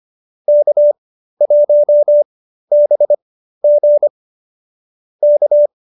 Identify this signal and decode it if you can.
Morse code, heard as a keyed tone, about 25 wpm, spaced as for 13 wpm, 590 Hz. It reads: K1BG K